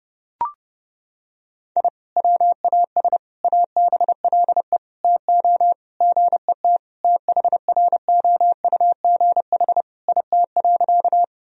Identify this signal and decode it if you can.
Morse code, heard as a keyed tone, about 30 wpm, 700 Hz. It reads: I WAS ABLE TO GET THROUGH IT.
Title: Sentences from Top 1000 Words - Review Part 2 - 30wpm